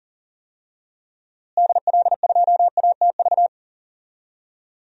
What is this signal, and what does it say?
Morse code, heard as a keyed tone, about 40 wpm, 700 Hz. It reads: DL2ATV
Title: DXCC-100 Call Signs - 6 Characters - Repeated 3x - 40wpm